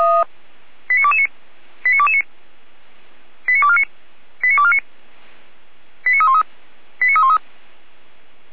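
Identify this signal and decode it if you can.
Five-tone selective call (ZVEI), heard as quick runs of five tones, tones of about 70 ms. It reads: 89229 89229 89249 89249 89313 89313
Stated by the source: ZVEI five-tone call received on SDR.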